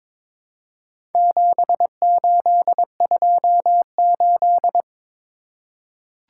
Morse code, heard as a keyed tone, about 23 wpm, 700 Hz. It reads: 7828
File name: practice / 22wpm / Four Letter-Number Combinations - 22wpm